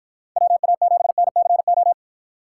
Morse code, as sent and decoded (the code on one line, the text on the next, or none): .-- .- --... -. -.-. ---
WA7NCO